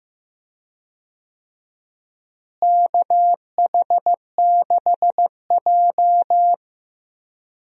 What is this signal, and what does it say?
Morse code, heard as a keyed tone, about 15 wpm, 700 Hz. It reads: KH6J
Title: DXCC-100 Call Signs - 4 Characters - Repeated 2x - 15wpm